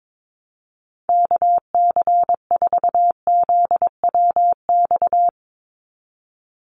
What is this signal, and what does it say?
Morse code, heard as a keyed tone, about 22 wpm, 700 Hz. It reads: KC4ZWX
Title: USA Call Signs 22wpm